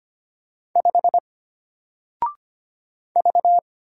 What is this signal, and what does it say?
Morse code, heard as a keyed tone, about 25 wpm, 700 Hz. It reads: 5 V
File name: Single-Letter-Number-25wpm